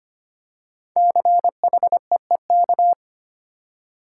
Morse code, heard as a keyed tone, about 25 wpm, 700 Hz. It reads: CHEEK
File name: 5 Letter Words - First 3 Letters Given - 25wpm